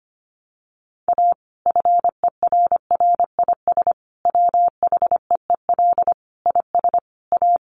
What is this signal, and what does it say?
Morse code, heard as a keyed tone, about 25 wpm, 700 Hz. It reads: A FERRIS WHEEL IS A
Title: Additional Easy Phrases 25wpm